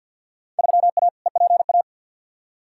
Morse code, heard as a keyed tone, about 49 wpm, 700 Hz. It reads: 3A EPA
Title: Field Day Exchanges - 50wpm